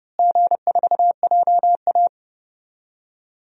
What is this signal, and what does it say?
Morse code, heard as a keyed tone, about 30 wpm, 700 Hz. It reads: G4JA
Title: DXCC-100 Call Signs - 4 Characters - Repeated 3x - 30wpm